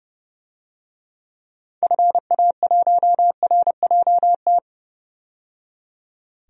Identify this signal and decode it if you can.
Morse code, heard as a keyed tone, about 30 wpm, 700 Hz. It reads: FA1RJT
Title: DXCC-100 Call Signs - 6 Characters - 30wpm